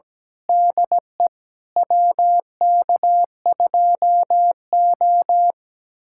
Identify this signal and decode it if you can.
Morse code, heard as a keyed tone, about 17 wpm, 700 Hz. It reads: DE WK2O
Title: USA Callsign DE Callsign - 4 Characters - 17wpm